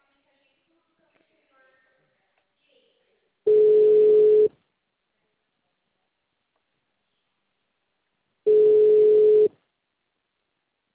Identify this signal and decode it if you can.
Ringback tone (Europe, 425 Hz), with speech.
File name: out- -2007- - - .11204